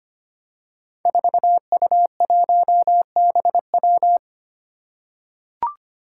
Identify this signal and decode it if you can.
Morse code, heard as a keyed tone, about 25 wpm, 700 Hz. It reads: 4U1BW
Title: Five Challenging Letter-Number Combinations 25wpm